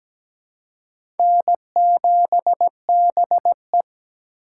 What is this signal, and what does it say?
Morse code, heard as a keyed tone, about 17 wpm, 700 Hz. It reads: N7BE
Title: USA Call Signs - 4 Characters - 3x Character Spacing - 17wpm